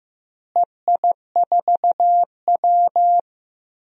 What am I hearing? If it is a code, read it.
Morse code, heard as a keyed tone, about 15 wpm, 700 Hz. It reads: EI4W
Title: DXCC-100 Call Signs - 4 Characters - Repeated 3x - 15wpm